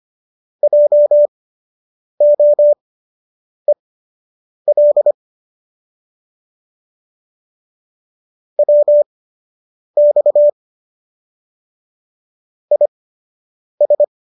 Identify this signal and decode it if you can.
Morse code, heard as a keyed tone, about 25 wpm, spaced as for 8 wpm, 590 Hz. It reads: JOEL WX IS